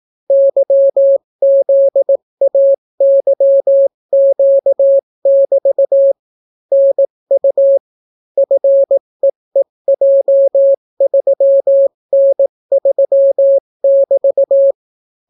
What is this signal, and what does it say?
Morse code, heard as a keyed tone, about 18 wpm, 550 Hz. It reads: YZAYQ<BT> NU FEEJ3N3<BT>